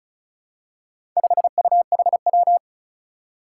Morse code, heard as a keyed tone, about 36 wpm, 700 Hz. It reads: 5UHW
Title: Four Challenging Letter-Number Combinations 35wpm